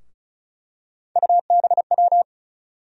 Morse code, heard as a keyed tone, about 36 wpm, 700 Hz. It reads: UBW
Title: USA Call Sign Postfixes - 3 Letters 35wpm